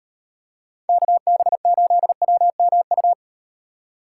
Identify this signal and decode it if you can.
Morse code, heard as a keyed tone, about 37 wpm, 700 Hz. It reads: KB8WMU